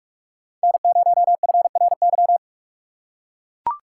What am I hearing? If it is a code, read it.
Morse code, heard as a keyed tone, about 42 wpm, 700 Hz. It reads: N0FRY